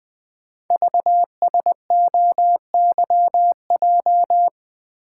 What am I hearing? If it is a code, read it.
Morse code, heard as a keyed tone, about 20 wpm, 700 Hz. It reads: VSOYJ
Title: Code Groups - 5 Letters - 20wpm